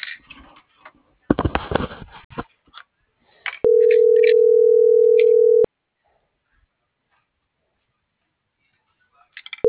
Ringback tone (North America), 440 + 480 Hz.